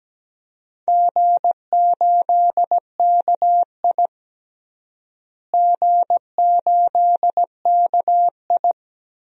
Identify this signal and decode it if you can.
Morse code, heard as a keyed tone, about 17 wpm, 700 Hz. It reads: G8KI G8KI
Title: DXCC-100 Call Signs - 4 Characters - Repeated 2x - 17wpm